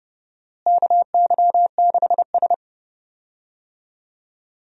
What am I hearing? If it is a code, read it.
Morse code, heard as a keyed tone, about 30 wpm, 700 Hz. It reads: KY6S
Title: USA Call Signs - 4 Characters - Repeated 3x - 30wpm